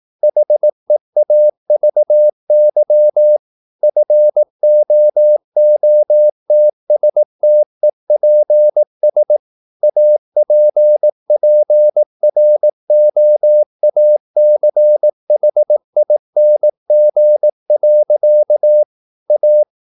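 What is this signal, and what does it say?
Morse code, heard as a keyed tone, about 18 wpm, 600 Hz. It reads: HEAVY FOOTSTEPS APPROACHING. A
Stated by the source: Morse practice at 18 wpm, English text, from CW med Gnister.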